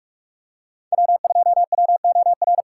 Morse code, heard as a keyed tone, about 44 wpm, 700 Hz. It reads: W2WOR